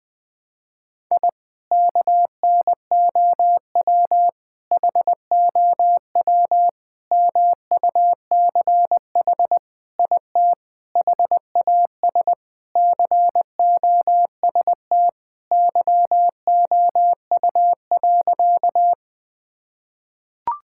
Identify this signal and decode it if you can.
Morse code, heard as a keyed tone, about 20 wpm, 700 Hz. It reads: I KNOW HOW MUCH IT HAS COST YOU.
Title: Sentences from Top 1000 Words - Review Part 5 - 20wpm